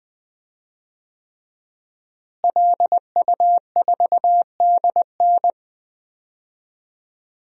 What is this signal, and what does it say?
Morse code, heard as a keyed tone, about 20 wpm, 700 Hz. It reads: LU4DN
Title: DXCC-100 Call Signs - 5 Characters - Repeated 3x - 20wpm